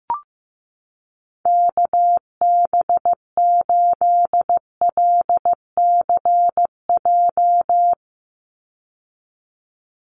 Morse code, heard as a keyed tone, about 15 wpm, 700 Hz. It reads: KB8LCJ
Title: USA Call Signs 15wpm